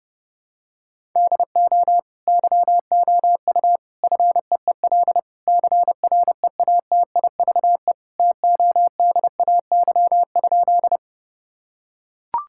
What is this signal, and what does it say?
Morse code, heard as a keyed tone, about 30 wpm, 700 Hz. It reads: DO YOU FEEL CREATIVE TODAY?
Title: Sentences from Top 1000 Words - Review Part 5 - 30wpm